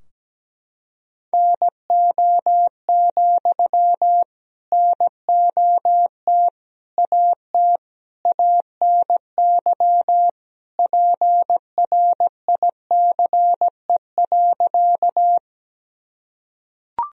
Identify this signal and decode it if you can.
Morse code, heard as a keyed tone, about 17 wpm, 700 Hz. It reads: NO, NOT AT ANY PRICE.
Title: Sentences from Top 1000 Words - Review Part 4 - 17wpm